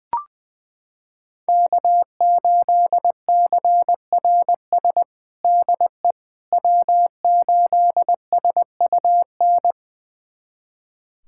Morse code, heard as a keyed tone, about 20 wpm, 700 Hz. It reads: K8CRS DE W8SUN
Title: USA Callsign DE Callsign - 5 Characters - 20wpm